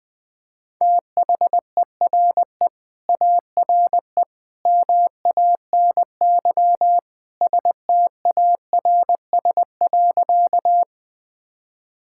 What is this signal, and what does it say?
Morse code, heard as a keyed tone, about 21 wpm, 700 Hz. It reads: THERE ARE MANY STARS.
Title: Sentences with 2-Word Phrases - 20wpm